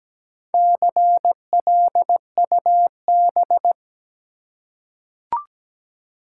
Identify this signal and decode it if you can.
Morse code, heard as a keyed tone, about 17 wpm, 700 Hz. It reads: CLUB
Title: Four Letter Words - 17wpm